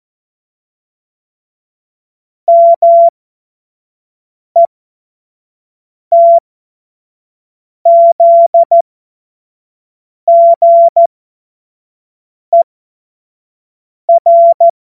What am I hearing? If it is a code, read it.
Morse code, heard as a keyed tone, about 14 wpm, spaced as for 5 wpm, 680 Hz. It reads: METZGER